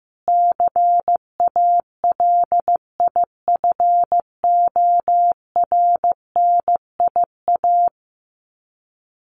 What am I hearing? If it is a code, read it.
Morse code, heard as a keyed tone, about 15 wpm, 700 Hz. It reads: CALIFORNIA
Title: Sets of 5 Words Encore2 15wpm